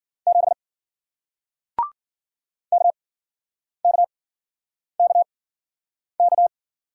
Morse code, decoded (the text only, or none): 7 K K K K